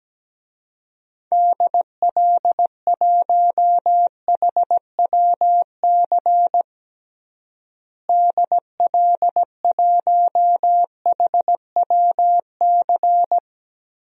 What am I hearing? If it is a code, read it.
Morse code, heard as a keyed tone, about 17 wpm, 700 Hz. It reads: DL1HWC DL1HWC
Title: DXCC-100 Call Signs - 6 Characters - Repeated 3x - 17wpm